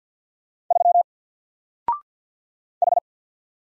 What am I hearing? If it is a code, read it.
Morse code, heard as a keyed tone, about 48 wpm, 700 Hz. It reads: 3 H
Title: Single Letter-Number Alt 50wpm